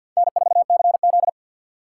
Morse code, heard as a keyed tone, about 50 wpm, 700 Hz. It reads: N4CZ
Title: USA Call Signs - 4 Characters - 6x Character Spacing - 50wpm